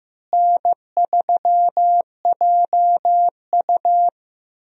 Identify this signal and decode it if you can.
Morse code, heard as a keyed tone, about 15 wpm, 700 Hz. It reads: N3JU
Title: USA Call Signs - 4 Characters - Repeated 3x - 15wpm